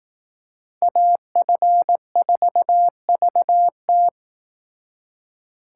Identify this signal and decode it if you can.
Morse code, heard as a keyed tone, about 19 wpm, 700 Hz. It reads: AF4VT